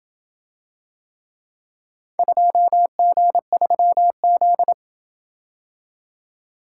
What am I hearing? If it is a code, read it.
Morse code, heard as a keyed tone, about 28 wpm, 700 Hz. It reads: 2G3Z